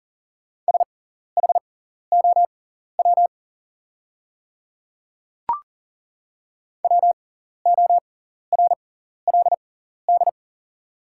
Morse code, heard as a keyed tone, about 40 wpm, spaced as for 14 wpm, 690 Hz. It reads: SHOW WORLD
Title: 6x Character Spacing - Top 200 Words - 40wpm